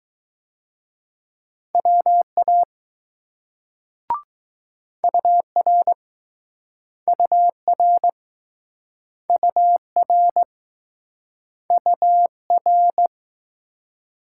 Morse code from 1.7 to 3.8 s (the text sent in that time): WA